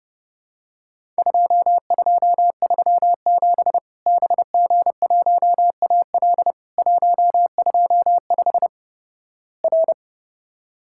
Morse code, heard as a keyed tone, about 32 wpm, 700 Hz. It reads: 2237 BG1AL 125 R
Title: Worked All Europe DX Contest - QTC Exchange - 30wpm